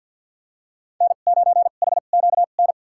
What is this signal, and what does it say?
Morse code, heard as a keyed tone, about 49 wpm, spaced as for 36 wpm, 700 Hz. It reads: N9HQN
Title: USA Call Signs - 5 Characters - 2x Character Spacing - 50wpm